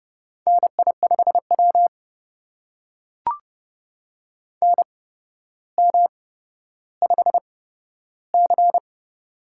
Morse code, heard as a keyed tone, about 32 wpm, spaced as for 13 wpm, 700 Hz. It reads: NI5W NM5C
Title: USA Call Signs - 4 Characters - 8x Character Spacing - 30wpm